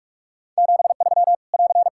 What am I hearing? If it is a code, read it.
Morse code, heard as a keyed tone, about 45 wpm, 700 Hz. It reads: 73 <AR>
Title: Practice QSO - Exchange 4 - Ending - 45wpm